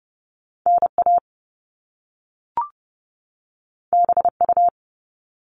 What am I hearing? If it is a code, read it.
Morse code, heard as a keyed tone, about 30 wpm, 700 Hz. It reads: NA BU
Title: Top Two Letter Combinations in Words 30wpm